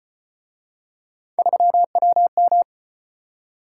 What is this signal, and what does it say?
Morse code, heard as a keyed tone, about 34 wpm, 700 Hz. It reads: 3WM